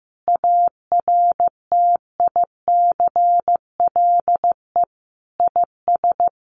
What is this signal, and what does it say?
Morse code, heard as a keyed tone, about 15 wpm, 700 Hz. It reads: ARTICLE IS
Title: Sets of 5 Words Encore2 15wpm